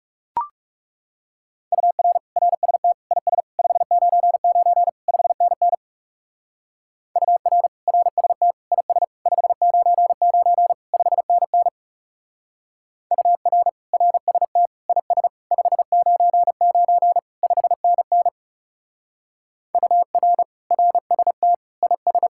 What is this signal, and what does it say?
Morse code, heard as a keyed tone, about 35 wpm, 700 Hz. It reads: UR RST IS 599 5NN UR RST IS 599 5NN UR RST IS 599 5NN UR RST IS
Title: Speed-Racing Common QSO Phrases Encore - 30wpm